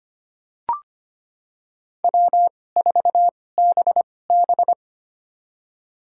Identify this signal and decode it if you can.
Morse code, heard as a keyed tone, about 25 wpm, spaced as for 18 wpm, 700 Hz. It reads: W4BB
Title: USA Call Signs - 4 Characters - 2x Character Spacing - 25wpm